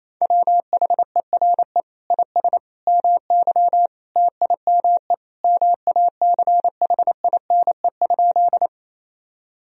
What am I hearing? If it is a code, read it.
Morse code, heard as a keyed tone, about 27 wpm, 700 Hz. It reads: WHERE IS MY TIME MACHINE?
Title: Sentences from Top 1000 Words - Review Part 2 - 28wpm